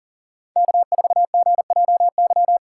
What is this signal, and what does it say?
Morse code, heard as a keyed tone, about 51 wpm, 700 Hz. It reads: K4GJY